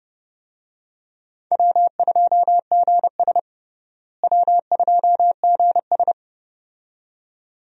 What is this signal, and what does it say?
Morse code, heard as a keyed tone, about 32 wpm, 700 Hz. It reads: W2GS W2GS
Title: USA Call Signs - 4 Characters - Repeated 2x - 30wpm